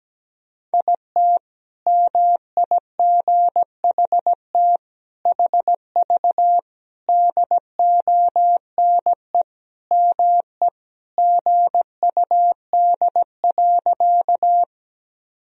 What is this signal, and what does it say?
Morse code, heard as a keyed tone, about 17 wpm, 700 Hz. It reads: IT MIGHT HV DONE ME GUD.
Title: Sentences from Top 500 Words and Abbreviations 17wpm